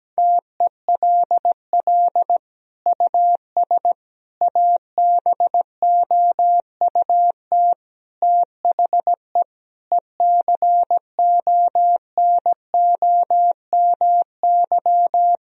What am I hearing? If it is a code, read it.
Morse code, heard as a keyed tone, about 17 wpm, 700 Hz. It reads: TELL US ABOUT THE ECONOMY
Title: Sentences from Top 1000 Words - Review Part 3 - 17wpm